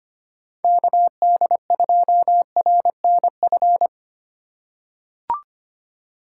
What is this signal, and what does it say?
Morse code, heard as a keyed tone, about 25 wpm, 700 Hz. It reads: KD2RNF
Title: USA Call Signs - 4x Character Spacing - 25wpm